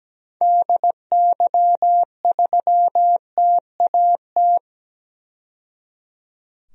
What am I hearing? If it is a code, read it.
Morse code, heard as a keyed tone, about 17 wpm, 700 Hz. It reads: DY3TAT
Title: DXCC-100 Call Signs - 6 Characters - 17wpm